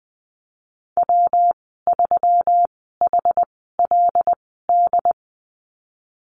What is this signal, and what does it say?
Morse code, heard as a keyed tone, about 19 wpm, spaced as for 14 wpm, 700 Hz. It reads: W3HLD
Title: 122 Lesson 20g - Number 3 - Call Signs 20wpm